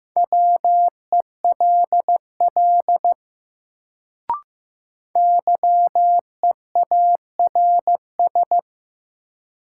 Morse code, heard as a keyed tone, about 15 wpm, 700 Hz. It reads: WELL YEARS
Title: Sentences from Top 200 Words - New Word Drill - 15wpm